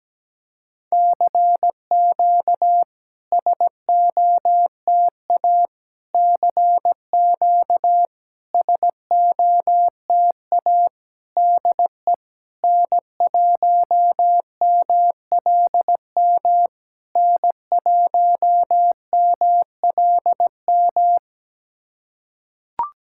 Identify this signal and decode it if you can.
Morse code, heard as a keyed tone, about 17 wpm, 700 Hz. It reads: CQ SOTA CQ SOTA DE N1MLM N1MLM
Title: SOTA QSO Practice 17wpm